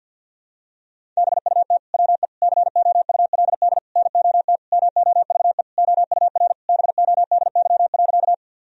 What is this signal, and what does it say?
Morse code, heard as a keyed tone, about 52 wpm, 700 Hz. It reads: BUT WE COULD NOT MOVE OUR BODY.
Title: Sentences from Top 1000 Words - Review Part 2 - 50wpm